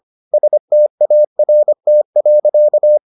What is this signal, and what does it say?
Morse code, heard as a keyed tone, about 25 wpm, 600 Hz. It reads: START.